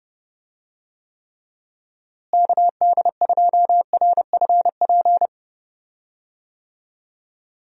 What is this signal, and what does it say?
Morse code, heard as a keyed tone, about 29 wpm, 700 Hz. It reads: KD2RFP
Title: USA Call Signs - 6 Characters - Repeated 3x - 30wpm